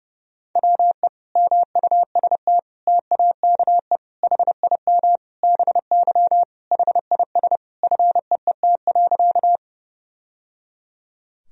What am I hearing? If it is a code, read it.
Morse code, heard as a keyed tone, about 30 wpm, 700 Hz. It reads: WE MUST TAKE HIM BY HIS FEET.
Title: Sentences from Top 1000 Words - Review Part 1 - 30wpm